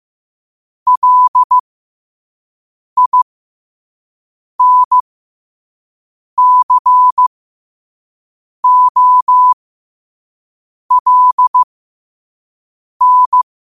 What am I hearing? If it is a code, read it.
Morse code, heard as a keyed tone, about 15 wpm, spaced as for 5 wpm, 1000 Hz. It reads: LINCOLN
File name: qotdfiles / 5wpm